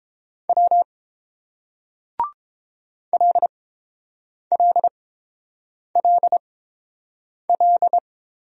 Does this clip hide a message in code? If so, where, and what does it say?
Yes: Morse code from 0.5 to 8.0 s, heard as a keyed tone, about 29 wpm, spaced as for 7 wpm, 700 Hz. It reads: W LLLL